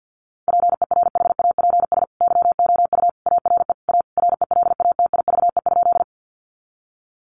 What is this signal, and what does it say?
Morse code, heard as a keyed tone, about 50 wpm, 700 Hz. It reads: PERHAPS YOU ARE A RELATIVE?
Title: Sentences from Top 1400 Words 50wpm